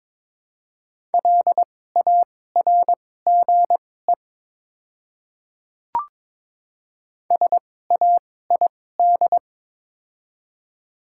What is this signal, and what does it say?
Morse code, heard as a keyed tone, about 23 wpm, spaced as for 16 wpm, 700 Hz. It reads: LARGE SAID
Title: 2x Character Spacing - Top 200 Words - 22wpm